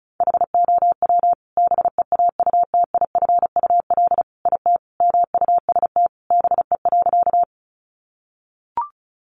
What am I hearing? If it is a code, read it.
Morse code, heard as a keyed tone, about 34 wpm, 700 Hz. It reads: HOW BEAUTIFUL IT MUST BE.
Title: Sentences from Top 900 Words 35wpm